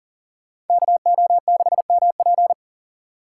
Morse code, heard as a keyed tone, about 50 wpm, 700 Hz. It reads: KO6MP
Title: Speed Racing - USA 2x2 Call Signs 40wpm